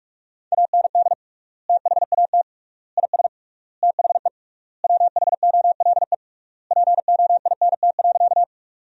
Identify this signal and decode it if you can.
Morse code, heard as a keyed tone, about 46 wpm, 700 Hz. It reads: AND THAT IS THE WHOLE POINT.
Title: 3x Word-Spacing - Sentences from Top 400 Words - 45wpm